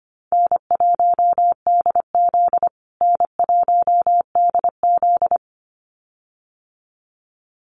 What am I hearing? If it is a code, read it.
Morse code, heard as a keyed tone, about 25 wpm, 700 Hz. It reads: N1DZ N1DZ
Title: USA Call Signs Repeated 2x 25wpm